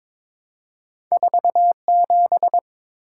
Morse code, heard as a keyed tone, about 22 wpm, 700 Hz. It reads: 47
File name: Two Letter-Number Combinations - 22wpm